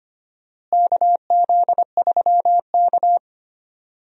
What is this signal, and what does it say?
Morse code, heard as a keyed tone, about 25 wpm, 700 Hz. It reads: KZ3K
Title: USA Call Signs - 4 Characters - Repeated 3x - 25wpm